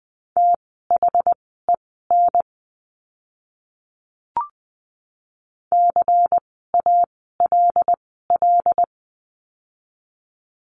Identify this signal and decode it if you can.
Morse code, heard as a keyed tone, about 20 wpm, spaced as for 14 wpm, 700 Hz. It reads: THEN CALL
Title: 108 Lesson 18f - Letter m - Words 20wpm